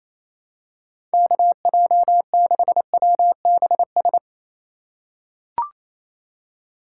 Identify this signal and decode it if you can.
Morse code, heard as a keyed tone, about 28 wpm, 690 Hz. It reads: KJ6WBS
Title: USA Call Signs - 6 Characters - 2x Character Spacing - 28wpm